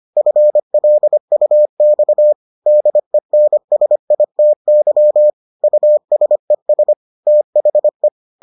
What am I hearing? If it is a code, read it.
Morse code, heard as a keyed tone, about 25 wpm, 600 Hz. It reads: FLUX DENSITY USES THE